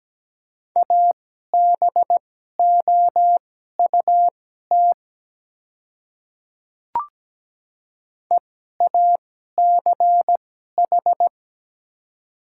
Morse code, heard as a keyed tone, about 17 wpm, spaced as for 12 wpm, 690 Hz. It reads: ABOUT EACH
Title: 2x Character Spacing - Top 100 Words - 17wpm